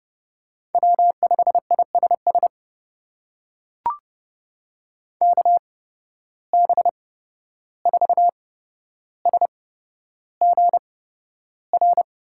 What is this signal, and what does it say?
Morse code, heard as a keyed tone, about 29 wpm, spaced as for 12 wpm, 700 Hz. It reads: W5ISS KB4SGR
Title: USA Call Signs - 8x Character Spacing - 30wpm